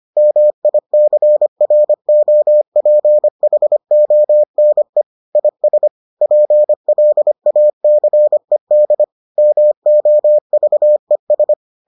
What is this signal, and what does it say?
Morse code, heard as a keyed tone, about 25 wpm, 600 Hz. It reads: MICROPHONE IS PLACED MOVES